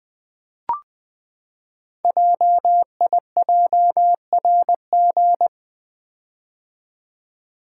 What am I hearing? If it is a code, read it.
Morse code, heard as a keyed tone, about 20 wpm, 700 Hz. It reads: JIJRG
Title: Code Groups - 5 Letters - 20wpm